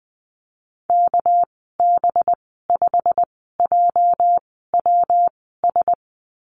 Morse code, heard as a keyed tone, about 20 wpm, spaced as for 14 wpm, 700 Hz. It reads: KB5JWS
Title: 219 Lesson 35g - Number 0 - Call Signs 20wpm